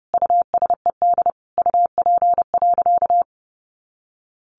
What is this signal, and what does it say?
Morse code, heard as a keyed tone, about 28 wpm, 700 Hz. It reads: USED UP.
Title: Sentences from Top 900 Words 30wpm